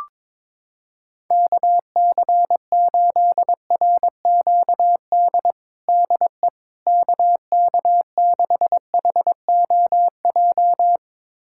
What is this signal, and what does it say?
Morse code, heard as a keyed tone, about 22 wpm, 700 Hz. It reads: KC8RQD DE KK6HOJ
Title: USA Callsign DE Callsign - 6 Characters - 22wpm